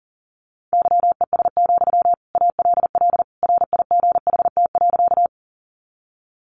Morse code, heard as a keyed tone, about 40 wpm, 700 Hz. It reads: YES, ALL RIGHT.
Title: Sentences from Top 500 Words 40wpm